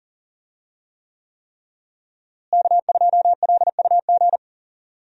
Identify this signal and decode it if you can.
Morse code, heard as a keyed tone, about 40 wpm, 700 Hz. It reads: K2LUG
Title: USA Call Signs - 5 Characters - Repeated 3x - 40wpm